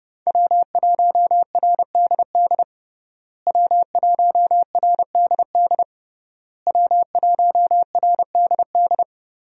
Morse code, heard as a keyed tone, about 30 wpm, 700 Hz. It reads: W1RDD W1RDD W1RDD
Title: USA Call Signs - 5 Characters - Repeated 3x - 30wpm